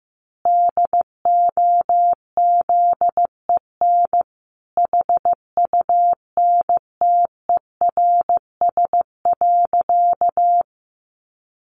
Morse code, heard as a keyed tone, about 15 wpm, 700 Hz. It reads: DOZEN HUNTERS.
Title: A Sound Of Thunder 15wpm